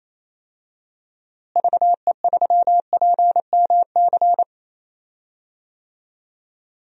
Morse code, heard as a keyed tone, about 28 wpm, 700 Hz. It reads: VE3PMC
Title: DXCC-100 Call Signs - 6 Characters - Repeated 3x - 28wpm